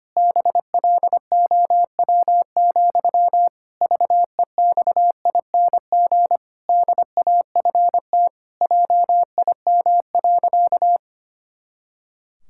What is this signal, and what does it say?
Morse code, heard as a keyed tone, about 25 wpm, 700 Hz. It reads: BLOW, VEXING DAFT JIM.
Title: Pangram Challenge v2 - 25wpm